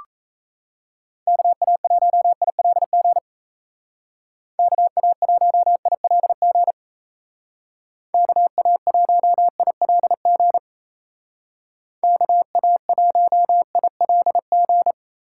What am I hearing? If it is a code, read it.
Morse code, heard as a keyed tone, about 37 wpm, 700 Hz. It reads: KA1ILG KA1ILG KA1ILG KA1ILG